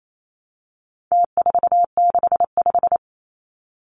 Morse code, heard as a keyed tone, about 28 wpm, 700 Hz. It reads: T465
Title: Numbers - Contest Style - 4-Digit 28wpm